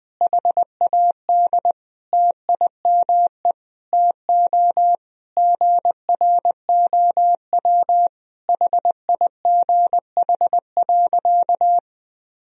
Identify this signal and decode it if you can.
Morse code, heard as a keyed tone, about 20 wpm, 700 Hz. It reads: HAD TIME TO GROW HIGH.
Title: Sentences from Top 1000 Words - Review Part 1 - 20wpm